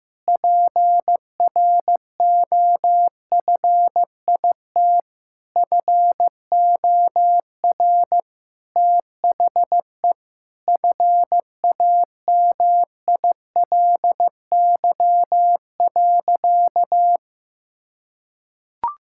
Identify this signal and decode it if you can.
Morse code, heard as a keyed tone, about 16 wpm, 700 Hz. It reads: PROFIT FOR THE FAMILY.
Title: Sentences from Top 1000 Words - Review Part 5 - 15wpm